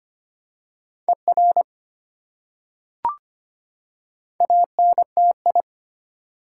Morse code, heard as a keyed tone, about 25 wpm, 700 Hz. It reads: ER ANTI
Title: Prefixes and Suffixes 25wpm